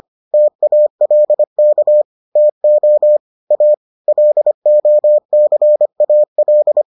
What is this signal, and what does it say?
Morse code, heard as a keyed tone, about 25 wpm, 600 Hz. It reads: TALK TO A LOCAL